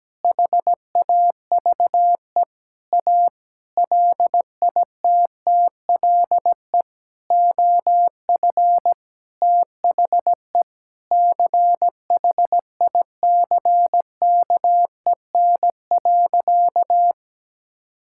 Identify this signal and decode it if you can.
Morse code, heard as a keyed tone, about 17 wpm, 700 Hz. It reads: HAVE A LITTLE OF THE CHICKEN.
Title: Sentences from Top 1000 Words - Review Part 5 - 17wpm